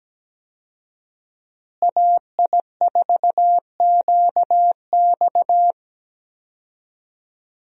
Morse code, heard as a keyed tone, about 17 wpm, 700 Hz. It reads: AI4QX
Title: USA Call Signs - 5 Characters - Repeated 2x - 17wpm